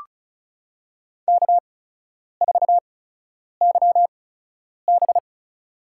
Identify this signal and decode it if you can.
Morse code, heard as a keyed tone, about 36 wpm, spaced as for 10 wpm, 700 Hz. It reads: K4YB
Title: USA Call Signs - 8x Character Spacing - 35wpm